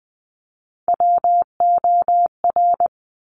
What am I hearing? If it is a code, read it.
Morse code, heard as a keyed tone, about 20 wpm, 700 Hz. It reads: WOR
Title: Top Three Letter Combinations in Words 20wpm